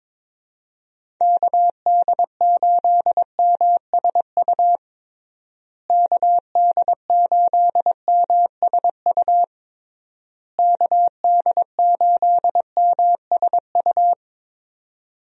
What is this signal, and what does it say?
Morse code, heard as a keyed tone, about 22 wpm, 700 Hz. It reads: KD8MSU KD8MSU KD8MSU
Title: USA Call Signs - 6 Characters - Repeated 3x - 22wpm